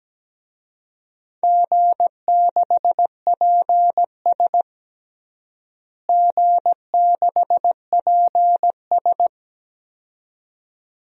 Morse code, heard as a keyed tone, about 17 wpm, 700 Hz. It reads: G6PS G6PS
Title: DXCC-100 Call Signs - 4 Characters - Repeated 2x - 17wpm